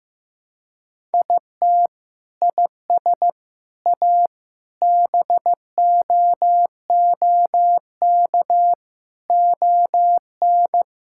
Morse code, heard as a keyed tone, about 15 wpm, 700 Hz. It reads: IT IS A BOOK ON
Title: Sentences from Top 1000 Words - Review Part 5 - 15wpm